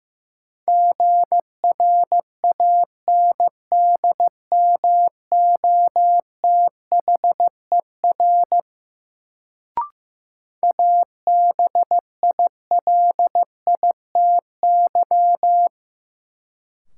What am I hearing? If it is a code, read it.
Morse code, heard as a keyed tone, about 15 wpm, 700 Hz. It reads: GRANDMOTHER ABILITY
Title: Sentences from Top 1200 Words - New Word Drill Alt - 15wpm